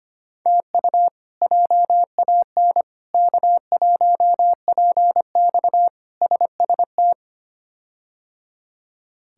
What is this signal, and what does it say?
Morse code, heard as a keyed tone, about 25 wpm, 700 Hz. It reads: TU JAN K1PX SST
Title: K1USN SST QSO Practice - 25wpm